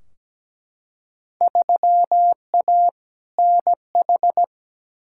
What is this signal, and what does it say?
Morse code, heard as a keyed tone, about 17 wpm, 700 Hz. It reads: 3A NH
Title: Field Day Exchanges Encore - 17wpm